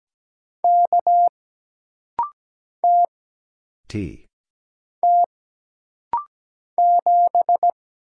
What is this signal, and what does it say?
Morse code, heard as a keyed tone, about 17 wpm, 700 Hz. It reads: K T T 7
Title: Single Letter-Number Alt - Rapid-Fire 17wpm